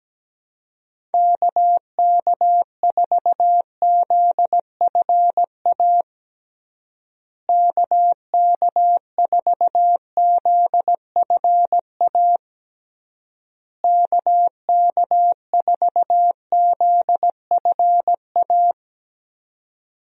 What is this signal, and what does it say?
Morse code, heard as a keyed tone, about 17 wpm, 700 Hz. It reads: KK4ZFA KK4ZFA KK4ZFA
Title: USA Call Signs - 6 Characters - Repeated 3x - 17wpm